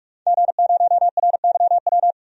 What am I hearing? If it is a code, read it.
Morse code, heard as a keyed tone, about 46 wpm, 700 Hz. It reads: G0RYW
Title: DXCC-100 Call Signs - 5 Characters - 45wpm